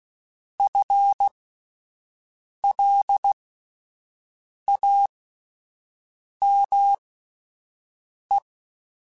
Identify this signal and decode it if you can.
Morse code, heard as a keyed tone, about 16 wpm, spaced as for 5 wpm, 800 Hz. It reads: FLAME
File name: Spaced / FL Runs sp mp3s / FL Runs sp @10